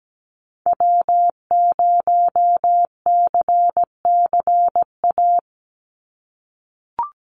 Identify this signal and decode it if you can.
Morse code, heard as a keyed tone, about 17 wpm, 700 Hz. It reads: W0CCA
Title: NA SOTA Activator Callsigns 17wpm